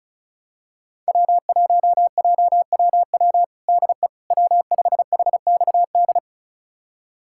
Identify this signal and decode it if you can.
Morse code, heard as a keyed tone, about 34 wpm, 700 Hz. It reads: W1JWW DE W5HXD
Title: USA Callsign DE Callsign 35wpm